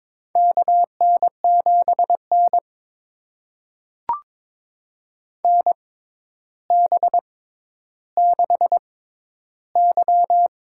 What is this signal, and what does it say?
Morse code, heard as a keyed tone, about 22 wpm, spaced as for 11 wpm, 700 Hz. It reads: KN7N NB6Y
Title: USA Call Signs - 6x Character Spacing - 22wpm